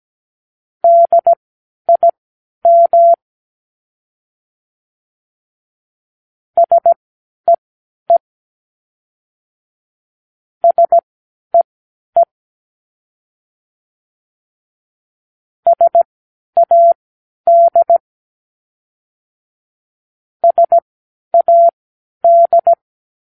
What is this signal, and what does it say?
Morse code, heard as a keyed tone, about 17 wpm, spaced as for 10 wpm, 680 Hz. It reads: DIM SEE SEE SAD SAD
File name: k7qo / Practicewords